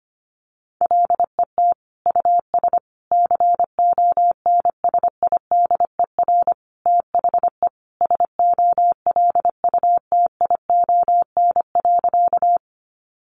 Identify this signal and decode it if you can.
Morse code, heard as a keyed tone, about 25 wpm, 700 Hz. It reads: LET US CONSIDER THE SOLUTION.
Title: Sentences from Top 1300 Words 25wpm